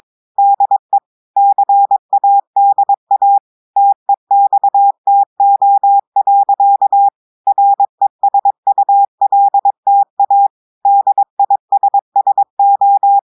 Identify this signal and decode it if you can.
Morse code, heard as a keyed tone, about 22 wpm, 800 Hz. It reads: DE CADA TEXTO. RESULTA DISSO
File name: audios / portugais / portugais texte en clair 1 vitesse 22 mots minutes